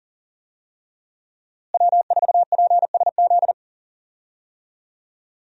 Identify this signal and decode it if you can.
Morse code, heard as a keyed tone, about 40 wpm, 700 Hz. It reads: W4PSZ
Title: USA Call Signs - 5 Characters - Repeated 3x - 40wpm